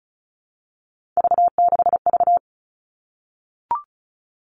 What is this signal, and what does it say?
Morse code, heard as a keyed tone, about 34 wpm, 700 Hz. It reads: V6V
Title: All Cut Numbers - 3-Digit 35wpm